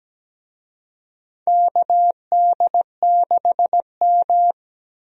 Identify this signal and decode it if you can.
Morse code, heard as a keyed tone, about 17 wpm, 700 Hz. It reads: KD6M
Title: USA Call Signs - 4 Characters - 17wpm